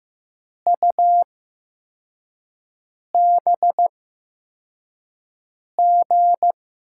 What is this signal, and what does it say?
Morse code, heard as a keyed tone, about 15 wpm, spaced as for 4 wpm, 700 Hz. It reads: UBG
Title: USA Call Signs - 6 Characters - 8x Character Spacing - 15wpm